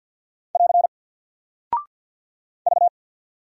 Morse code, heard as a keyed tone, about 49 wpm, 700 Hz. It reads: <AR> V
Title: Single Character (Letter:Number:Punctuation:Prosign) Alt 50wpm